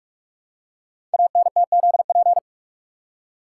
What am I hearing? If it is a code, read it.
Morse code, heard as a keyed tone, about 44 wpm, 690 Hz. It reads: ANTZP
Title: Code Groups - 5 Letters - 45wpm